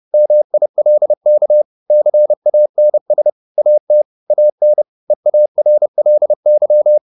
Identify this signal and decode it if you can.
Morse code, heard as a keyed tone, about 30 wpm, 600 Hz. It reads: MILK CANS AT AN EARLY
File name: the old man and the sea / the old man and the sea 1 30wpm